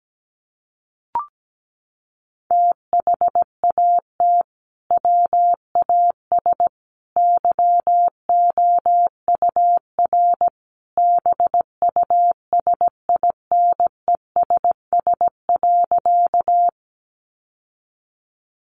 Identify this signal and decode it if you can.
Morse code, heard as a keyed tone, about 17 wpm, 700 Hz. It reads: THAT WAS YOUR BUSINESS.
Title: Sentences from Top 600 Words 17wpm